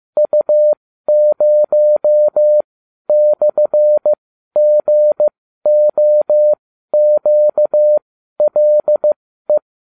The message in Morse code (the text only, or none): U0/GOQLE